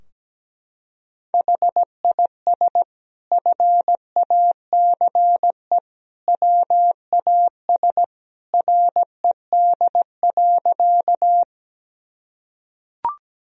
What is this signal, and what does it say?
Morse code, heard as a keyed tone, about 17 wpm, 700 Hz. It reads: HIS FACE WAS RED.
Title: Sentences from Top 1000 Words - Review Part 2 - 17wpm